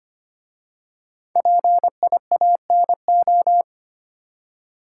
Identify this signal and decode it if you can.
Morse code, heard as a keyed tone, about 25 wpm, 700 Hz. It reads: PIANO
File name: 5 Letter Words - First 3 Letters Given - 25wpm